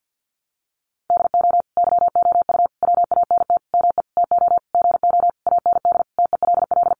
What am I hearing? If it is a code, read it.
Morse code, heard as a keyed tone, about 50 wpm, 700 Hz. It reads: DO YOU WANT ME TO GO AND TELL
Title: Sentences from Top 200 Words 50wpm